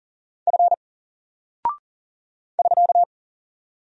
Morse code, heard as a keyed tone, about 40 wpm, 700 Hz. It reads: F <SK>
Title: Single Character (Letter:Number:Punctuation:Prosign) 40wpm